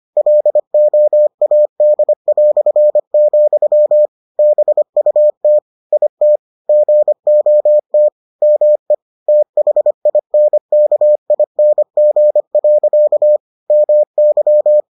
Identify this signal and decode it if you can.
Morse code, heard as a keyed tone, about 25 wpm, 600 Hz. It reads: LOAD", BUT IT GOT ME THINKING. MY